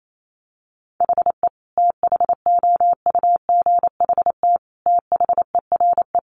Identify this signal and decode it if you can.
Morse code, heard as a keyed tone, about 28 wpm, 700 Hz. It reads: HE THOUGHT THERE
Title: Sentences from Top 1300 Words 28wpm